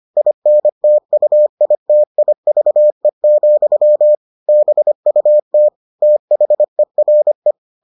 Morse code, heard as a keyed tone, about 25 wpm, 600 Hz. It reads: INTUITIVE, BUT THERE